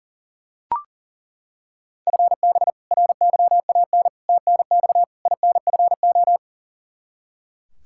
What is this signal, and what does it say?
Morse code, heard as a keyed tone, about 40 wpm, 700 Hz. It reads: FB RYAN TNX INFO
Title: K1USN SST QSO Practice 40wpm